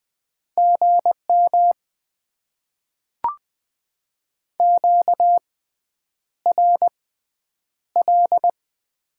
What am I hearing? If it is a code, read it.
Morse code, heard as a keyed tone, about 20 wpm, spaced as for 9 wpm, 690 Hz. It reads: GM QRL
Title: Top 20 QSO Elements - 6x Character Spacing - 20wpm